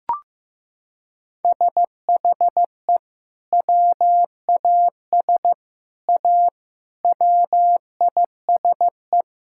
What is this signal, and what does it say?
Morse code, heard as a keyed tone, about 15 wpm, 700 Hz. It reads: SHE WAS A WISE
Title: Sentences from Top 1600 Words 15wpm